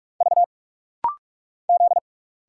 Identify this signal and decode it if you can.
Morse code, heard as a keyed tone, about 46 wpm, 700 Hz. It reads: V Z
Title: Single Letters - Rapid-Fire 45wpm